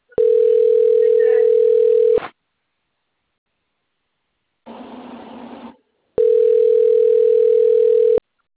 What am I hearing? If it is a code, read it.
Ringback tone (North America), 440 + 480 Hz.